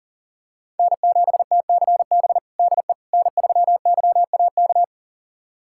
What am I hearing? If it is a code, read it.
Morse code, heard as a keyed tone, about 37 wpm, 700 Hz. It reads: N7TCB DE N3YAK
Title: USA Callsign DE Callsign - 5 Characters - 40wpm